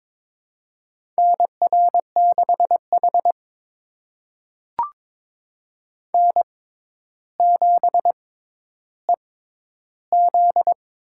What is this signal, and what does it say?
Morse code, heard as a keyed tone, about 22 wpm, spaced as for 11 wpm, 700 Hz. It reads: NR6H N7EZ
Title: USA Call Signs - 4 Characters - 6x Character Spacing - 22wpm